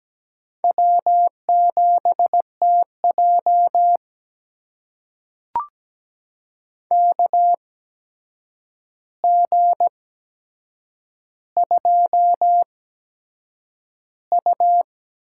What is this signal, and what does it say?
Morse code, heard as a keyed tone, about 17 wpm, spaced as for 7 wpm, 700 Hz. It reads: W7TJ KG2U
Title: USA Call Signs - 4 Characters - 8x Character Spacing - 17wpm